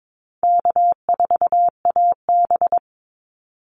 Morse code, heard as a keyed tone, about 22 wpm, 700 Hz. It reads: K4AB
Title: Easy Call Signs Repeated 3x - 22wpm